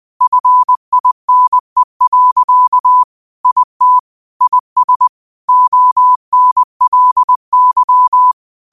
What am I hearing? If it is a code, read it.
Morse code, heard as a keyed tone, about 20 wpm, 1000 Hz. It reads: FINE. IT IS ONLY
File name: 20wpm